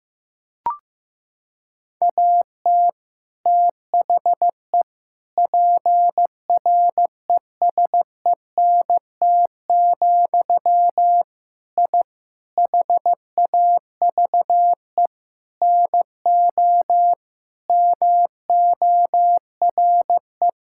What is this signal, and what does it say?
Morse code, heard as a keyed tone, about 16 wpm, 700 Hz. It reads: AT THE PRESENT, I HAVE NO MORE
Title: Sentences from Top 1000 Words - Review Part 5 - 15wpm